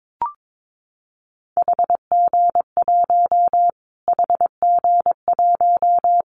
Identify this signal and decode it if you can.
Morse code, heard as a keyed tone, about 22 wpm, 700 Hz. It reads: HG1 HG1
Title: Callsign Prefixes for Top 25 DXCC - Repeated 3x - 22wpm